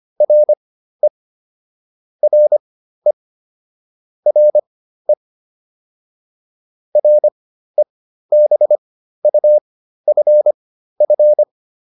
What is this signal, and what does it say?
Morse code, heard as a keyed tone, about 25 wpm, spaced as for 13 wpm, 600 Hz. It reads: RE RE RE REBUFF